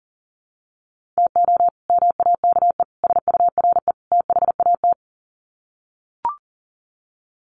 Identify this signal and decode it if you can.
Morse code, heard as a keyed tone, about 40 wpm, 700 Hz. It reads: TO MAKE SURE THAT
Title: Sets of 4 Words 40wpm